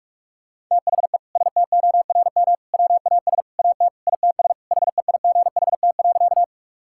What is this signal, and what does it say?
Morse code, heard as a keyed tone, about 46 wpm, 700 Hz. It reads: THE STORM WAS AT ITS HEIGHT.
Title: Sentences from Top 1600 Words 45wpm